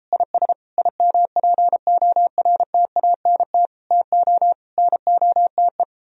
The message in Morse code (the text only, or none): IS IMPORTANT TO NOTE